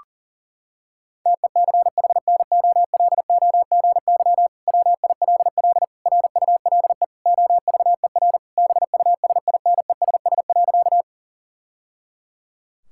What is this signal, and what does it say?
Morse code, heard as a keyed tone, about 40 wpm, 700 Hz. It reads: TECHNOLOGY WILL RULE OVER BUSINESS.
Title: Sentences from Top 1000 Words - Review Part 5 - 40wpm